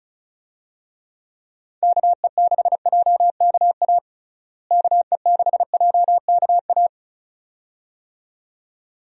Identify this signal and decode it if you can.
Morse code, heard as a keyed tone, about 36 wpm, 700 Hz. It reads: KE6JKA KE6JKA
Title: USA Call Signs - 6 Characters - Repeated 2x - 35wpm